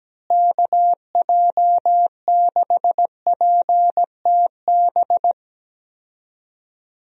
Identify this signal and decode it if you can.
Morse code, heard as a keyed tone, about 17 wpm, 700 Hz. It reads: KJ6PTB
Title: USA Call Signs - 6 Characters - Repeated 3x - 17wpm